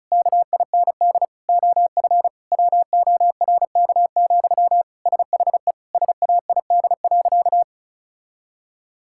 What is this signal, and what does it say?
Morse code, heard as a keyed tone, about 36 wpm, 700 Hz. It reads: KIND OF WORK, SHE SAID.
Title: Sentences from Top 1000 Words - Review Part 6 - 35wpm